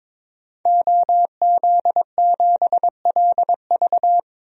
Morse code, heard as a keyed tone, about 22 wpm, 700 Hz. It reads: OZ7LV
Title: DXCC-100 Call Signs - 5 Characters - 22wpm